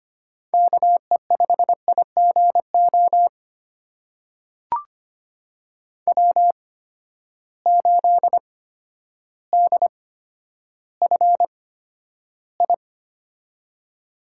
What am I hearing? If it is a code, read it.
Morse code, heard as a keyed tone, about 25 wpm, spaced as for 11 wpm, 700 Hz. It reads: KE5IGO W8DFI
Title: USA Call Signs - 8x Character Spacing - 25wpm